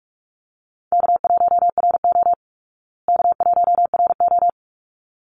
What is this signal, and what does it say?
Morse code, heard as a keyed tone, about 45 wpm, 700 Hz. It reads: K1RO K1RO
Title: Easy Call Signs Repeated 2x - 45wpm